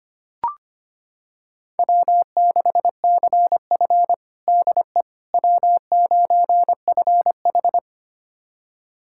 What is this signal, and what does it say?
Morse code, heard as a keyed tone, about 25 wpm, 700 Hz. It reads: W6CF DE W9FH
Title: USA Callsign DE Callsign - 4 Characters - 25wpm